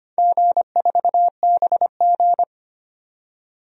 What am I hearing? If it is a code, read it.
Morse code, heard as a keyed tone, about 25 wpm, 700 Hz. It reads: G4BG
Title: DXCC-100 Call Signs - 4 Characters - 25wpm